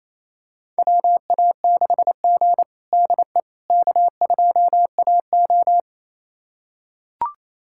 Morse code, heard as a keyed tone, about 28 wpm, 700 Hz. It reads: WA6G DE K2AO
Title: USA Callsign DE Callsign - 4 Characters - 28wpm